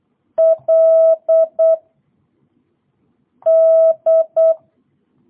Morse code, heard as a keyed tone, about 8 wpm, 660 Hz. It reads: L D